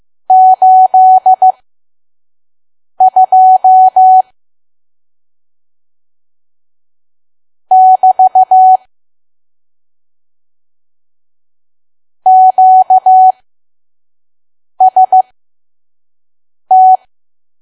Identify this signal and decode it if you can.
Morse code, heard as a keyed tone, about 15 wpm, spaced as for 5 wpm, 750 Hz. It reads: 82 <BT> QST
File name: Archive / 5 WPM files / PM